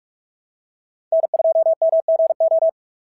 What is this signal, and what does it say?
Morse code, heard as a keyed tone, about 43 wpm, 650 Hz. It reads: N2MGO
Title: SOTA QSO Practice 45wpm